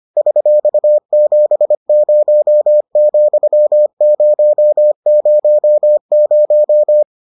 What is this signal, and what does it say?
Morse code, heard as a keyed tone, about 25 wpm, 600 Hz. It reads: $70,000